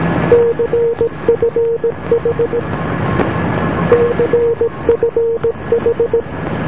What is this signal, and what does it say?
Morse code, heard as a keyed tone, about 17 wpm, 470 Hz, with noise. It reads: CFH CFH